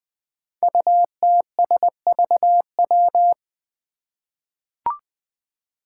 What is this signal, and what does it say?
Morse code, heard as a keyed tone, about 20 wpm, 700 Hz. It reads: UTSVW
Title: Five Challenging Letter-Number Combinations 20wpm